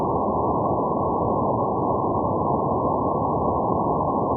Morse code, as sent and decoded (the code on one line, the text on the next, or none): none